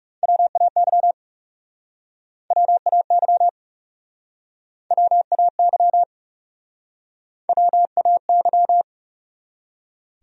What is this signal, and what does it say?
Morse code, heard as a keyed tone, about 40 wpm, 700 Hz. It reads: WAY WAY WAY WAY